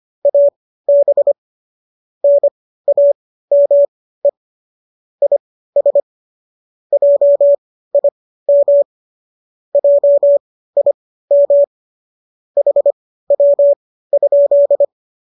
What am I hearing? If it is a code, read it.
Morse code, heard as a keyed tone, about 25 wpm, spaced as for 15 wpm, 570 Hz. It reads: AB NAME IS JIM JIM HW?
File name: qso203 15